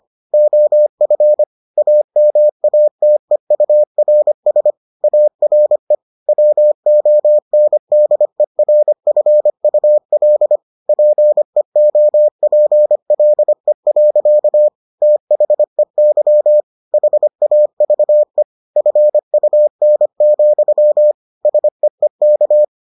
Morse code, heard as a keyed tone, about 25 wpm, 600 Hz. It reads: OF AMATEURS ARE WONDERFUL PEOPLE. THEY HAVE FUN, SEEK